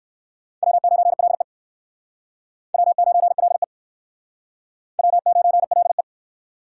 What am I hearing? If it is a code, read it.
Morse code, heard as a keyed tone, about 56 wpm, 700 Hz. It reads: W9LE W9LE W9LE